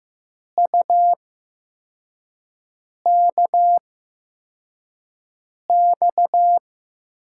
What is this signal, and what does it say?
Morse code, heard as a keyed tone, about 15 wpm, spaced as for 4 wpm, 700 Hz. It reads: UKX
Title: USA Call Signs - 5 Characters - 8x Character Spacing - 15wpm